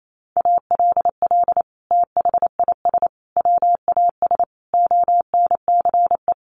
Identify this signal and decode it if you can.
Morse code, heard as a keyed tone, about 28 wpm, 700 Hz. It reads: ALL THIS WAS ONCE
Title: Sentences from Top 1500 Words 28wpm